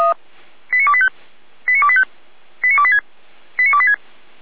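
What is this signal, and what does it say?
Five-tone selective call (ZVEI), heard as quick runs of five tones, tones of about 70 ms. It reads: 89285 89285 89286 89286
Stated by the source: ZVEI five-tone call received on an SDR, fire brigade Orlamünde (feuerwehr-orlamuende) recording archive.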